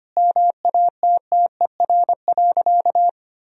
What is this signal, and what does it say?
Morse code, heard as a keyed tone, about 25 wpm, 700 Hz. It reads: MATTER.
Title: Sentences from Top 1000 Words - Review Part 2 - 25wpm